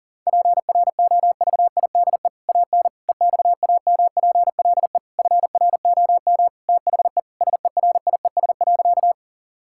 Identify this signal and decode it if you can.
Morse code, heard as a keyed tone, about 40 wpm, 700 Hz. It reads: PROVIDE AN EXAMPLE FROM THE SERIES.
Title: Sentences from Top 1000 Words - Review Part 5 - 40wpm